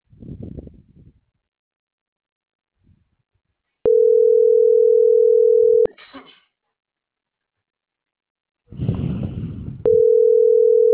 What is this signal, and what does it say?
Ringback tone (North America), 440 + 480 Hz.